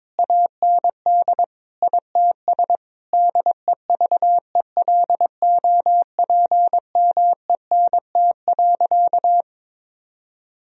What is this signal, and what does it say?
Morse code, heard as a keyed tone, about 22 wpm, 700 Hz. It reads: AND ITS DEVELOPMENT.
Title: Sentences from Top 1000 Words - Review Part 4 - 22wpm